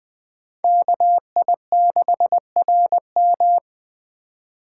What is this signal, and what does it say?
Morse code, heard as a keyed tone, about 23 wpm, 700 Hz. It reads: KI6RM
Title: Speed Racing - USA 2x2 Call Signs 20wpm